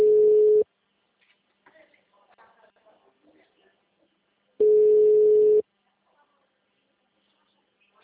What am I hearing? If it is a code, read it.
Ringback tone (Europe, 425 Hz).